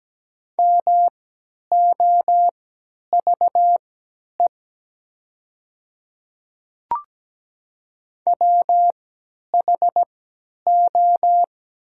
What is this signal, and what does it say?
Morse code, heard as a keyed tone, about 17 wpm, spaced as for 10 wpm, 700 Hz. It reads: MOVE WHO
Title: 3x Character Spacing - Top 200 Words - 17wpm